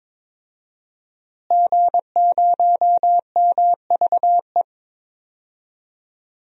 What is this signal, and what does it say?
Morse code, heard as a keyed tone, about 22 wpm, 700 Hz. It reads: G0MVE
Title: DXCC-100 Call Signs - 5 Characters - Repeated 3x - 22wpm